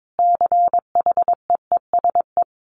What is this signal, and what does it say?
Morse code, heard as a keyed tone, about 22 wpm, 700 Hz. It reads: CHEESE
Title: Dit Challenge 22wpm